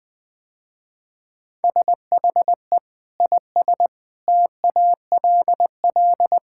Morse code, heard as a keyed tone, about 21 wpm, 700 Hz. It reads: SHE IS TALL
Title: Sentences with 2-Word Phrases - Spoken Beginning - 20wpm